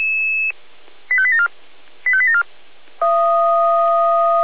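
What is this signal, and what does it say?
Five-tone selective call (ZVEI), heard as quick runs of five tones, tones of about 70 ms. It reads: 85874 85874
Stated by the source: ZVEI five-tone selective call 85875 received on an SDR, from Feuerwehr Orlamünde.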